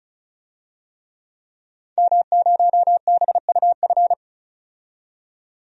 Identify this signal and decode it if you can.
Morse code, heard as a keyed tone, about 36 wpm, 700 Hz. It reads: M0BUF
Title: DXCC-100 Call Signs - 5 Characters - 35wpm